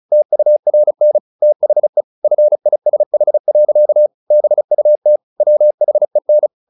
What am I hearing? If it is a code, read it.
Morse code, heard as a keyed tone, about 34 wpm, 600 Hz. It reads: TURN THE FISH. BUT WHEN
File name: the old man and the sea / the old man and the sea 1 35wpm